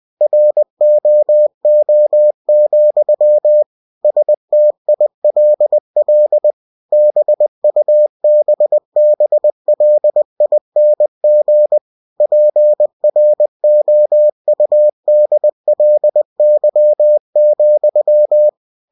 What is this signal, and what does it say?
Morse code, heard as a keyed tone, about 20 wpm, 600 Hz. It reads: ROO, STILL BUBBLING PROUDLY,
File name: cw-books / winnie-the-pooh / winnie-20wpm-page-0087